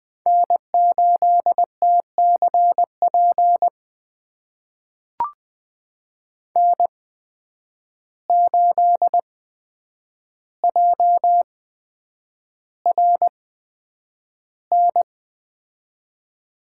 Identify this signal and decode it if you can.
Morse code, heard as a keyed tone, about 20 wpm, spaced as for 9 wpm, 700 Hz. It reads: N8TCP N8JRN
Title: USA Call Signs - 5 Characters - 8x Character Spacing - 20wpm